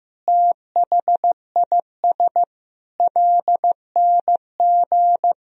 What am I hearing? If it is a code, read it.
Morse code, heard as a keyed tone, about 15 wpm, 700 Hz. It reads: THIS LNG
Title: Sentences from Top 300 Words and Abbreviations 15wpm